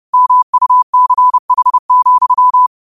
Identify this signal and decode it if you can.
Morse code, heard as a keyed tone, about 30 wpm, 1000 Hz. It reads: MUCH,
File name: qotdfiles / 30wpm